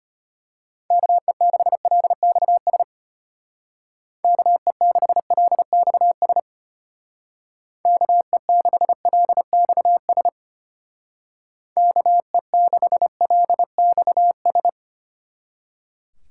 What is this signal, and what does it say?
Morse code, heard as a keyed tone, about 30 wpm, 700 Hz. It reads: KE6LXS KE6LXS KE6LXS KE6LXS
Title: Speed Racing - USA 2x3 Call Signs 25wpm